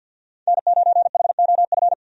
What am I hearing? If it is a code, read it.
Morse code, heard as a keyed tone, about 50 wpm, 700 Hz. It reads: N9HOF
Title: USA Call Signs - 5 Characters - 50wpm